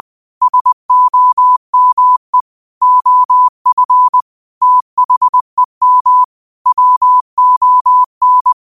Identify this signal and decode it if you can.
Morse code, heard as a keyed tone, about 20 wpm, 1000 Hz. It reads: SOME OF THEM WON
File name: qotdfiles / 20wpm